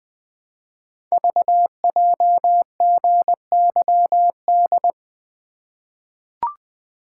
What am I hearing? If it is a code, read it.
Morse code, heard as a keyed tone, about 20 wpm, 700 Hz. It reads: VJGYD
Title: Code Groups - 5 Letters - 20wpm